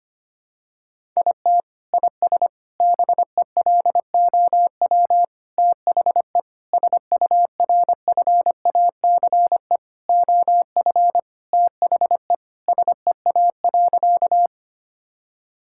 Morse code, heard as a keyed tone, about 25 wpm, 700 Hz. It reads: IT IS BELOW THE SURFACE OF THE SEA.
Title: Sentences from Top 1000 Words - Review Part 5 - 25wpm